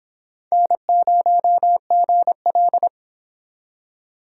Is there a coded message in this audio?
Yes: Morse code from 0.5 to 2.9 s, heard as a keyed tone, about 26 wpm, 700 Hz. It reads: N0GL